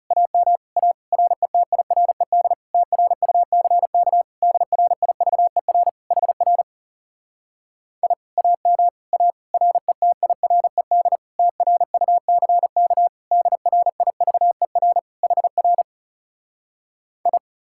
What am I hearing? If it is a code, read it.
Morse code, heard as a keyed tone, about 40 wpm, 700 Hz. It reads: AM A RETIRED TRUCK DRIVER HR I AM A RETIRED TRUCK DRIVER HR I